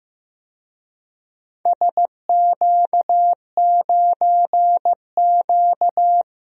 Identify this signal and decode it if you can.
Morse code, heard as a keyed tone, about 15 wpm, 700 Hz. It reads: SQ9Q
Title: DXCC-100 Call Signs - 4 Characters - Repeated 3x - 15wpm